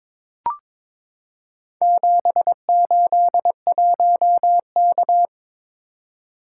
Morse code, heard as a keyed tone, about 22 wpm, 700 Hz. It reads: 781K
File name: practice / 22wpm / Four Letter-Number Combinations - 22wpm